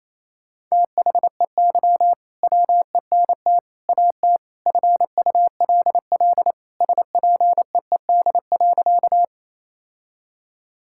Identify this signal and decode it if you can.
Morse code, heard as a keyed tone, about 27 wpm, 700 Hz. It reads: THEY WENT AT FULL SPEED.
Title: Sentences from Top 1000 Words - Review Part 5 - 28wpm